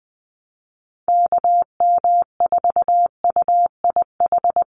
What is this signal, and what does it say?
Morse code, heard as a keyed tone, about 20 wpm, 700 Hz. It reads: KM4UIH
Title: USA Call Signs 20wpm